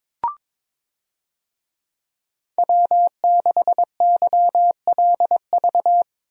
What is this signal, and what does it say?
Morse code, heard as a keyed tone, about 22 wpm, 700 Hz. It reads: W6YLV
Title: USA Call Signs - 5 Characters - 22wpm